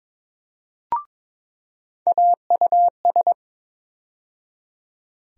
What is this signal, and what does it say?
Morse code, heard as a keyed tone, about 22 wpm, 700 Hz. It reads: AUS
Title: USA Call Sign Postfixes - 3 Letters 22wpm